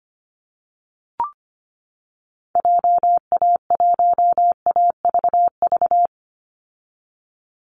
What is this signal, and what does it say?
Morse code, heard as a keyed tone, about 25 wpm, 700 Hz. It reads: JA1AVV
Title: USA Call Signs 25wpm